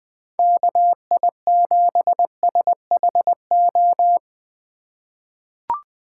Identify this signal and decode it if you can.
Morse code, heard as a keyed tone, about 20 wpm, 700 Hz. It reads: KI7SHO
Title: USA Call Signs - 6 Characters - 4x Character Spacing - 20wpm